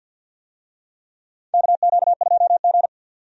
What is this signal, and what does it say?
Morse code, heard as a keyed tone, about 49 wpm, 700 Hz. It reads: KQ2G
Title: USA Call Signs - 4 Characters - 50wpm